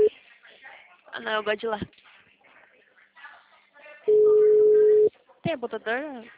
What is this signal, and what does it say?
Ringback tone (Europe, 425 Hz), with speech.